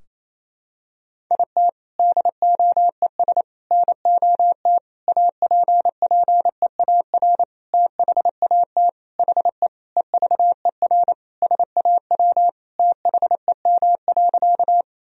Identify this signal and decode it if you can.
Morse code, heard as a keyed tone, about 28 wpm, 700 Hz. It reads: IT DOES NOT APPEAR THAT HE EVER SAW THEM.
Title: Sentences from Top 1000 Words - Review Part 2 - 28wpm